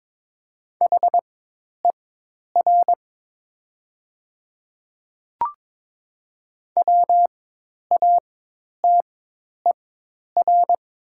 Morse code, heard as a keyed tone, about 22 wpm, spaced as for 10 wpm, 700 Hz. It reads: HER WATER
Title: 4x Character Spacing - Top 100 Words - 22wpm